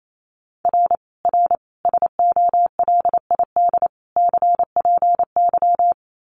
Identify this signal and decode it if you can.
Morse code, heard as a keyed tone, about 28 wpm, 700 Hz. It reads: R R SOLID CPY
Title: Common QSO Phrases 28wpm